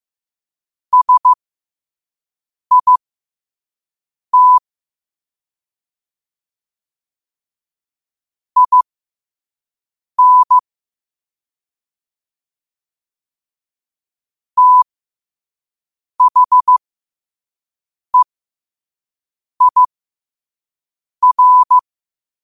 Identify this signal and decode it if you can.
Morse code, heard as a keyed tone, about 15 wpm, spaced as for 5 wpm, 1000 Hz. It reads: SIT IN THEIR
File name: qotdfiles / 5wpm